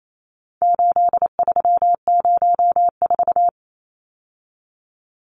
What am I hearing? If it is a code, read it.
Morse code, heard as a keyed tone, about 28 wpm, 700 Hz. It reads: 8304
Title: Numbers - 4-Digit 28wpm